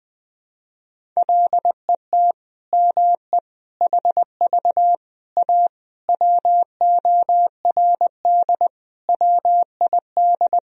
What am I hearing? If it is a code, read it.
Morse code, heard as a keyed tone, about 20 wpm, 700 Hz. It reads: LET ME HV A WORD WID
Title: Sentences from Top 300 Words and Abbreviations 20wpm